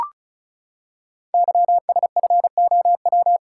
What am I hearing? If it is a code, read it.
Morse code, heard as a keyed tone, about 36 wpm, 700 Hz. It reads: YSFOW
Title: Code Groups - 5 Letter:Numbers - 35wpm